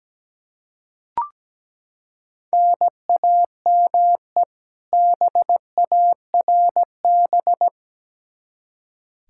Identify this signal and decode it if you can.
Morse code, heard as a keyed tone, about 17 wpm, 700 Hz. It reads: NAME BARB
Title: K1USN SST QSO Practice 17wpm